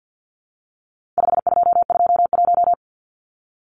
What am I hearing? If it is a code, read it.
Morse code, heard as a keyed tone, about 50 wpm, 700 Hz. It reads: 5221
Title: Numbers - 4-Digit 50wpm